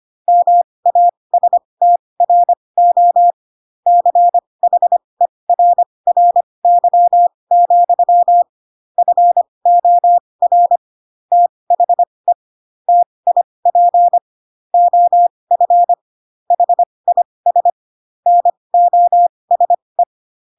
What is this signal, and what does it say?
Morse code, heard as a keyed tone, about 25 wpm, spaced as for 20 wpm, 700 Hz. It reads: MASTRO CHERRY, FOR THE TIP OF HIS NOSE
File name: Pinocchio / Pinocchio ch1 15 25 ik1hge